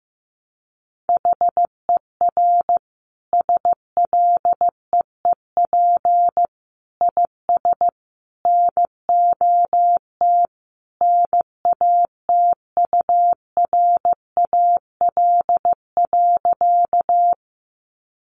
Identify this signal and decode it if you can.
Morse code, heard as a keyed tone, about 15 wpm, 700 Hz. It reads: HER SLEEP IS NOT NATURAL.
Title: Sentences from Top 700 Words 15wpm